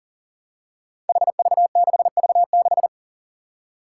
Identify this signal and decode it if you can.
Morse code, heard as a keyed tone, about 37 wpm, 700 Hz. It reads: HV6V6
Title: Five Challenging Letter-Number Combinations 40wpm